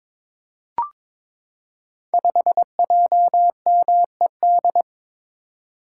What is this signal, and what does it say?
Morse code, heard as a keyed tone, about 22 wpm, 700 Hz. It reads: 5JMED
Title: Code Groups - 5 Letter:Numbers - 22wpm